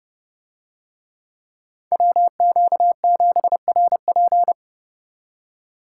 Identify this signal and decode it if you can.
Morse code, heard as a keyed tone, about 31 wpm, 700 Hz. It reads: WQ7RP